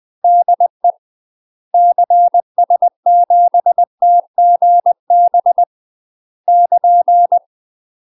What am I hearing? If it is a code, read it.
Morse code, heard as a keyed tone, about 20 wpm, 700 Hz. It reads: DE CS7TGB <KN>
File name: UK MASTER QSOs TRAINING / 20WPM / UK MASTER QSOs 0070